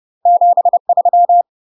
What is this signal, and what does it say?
Morse code, heard as a keyed tone, about 30 wpm, 700 Hz. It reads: 73
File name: UK MASTER QSOs TRAINING / 30WPM / UK MASTER QSOs 0019